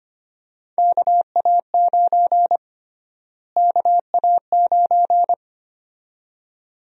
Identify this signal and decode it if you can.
Morse code, heard as a keyed tone, about 25 wpm, 700 Hz. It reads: KA9 KA9
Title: USA Call Sign Prefixes - Repeated 2x - 25wpm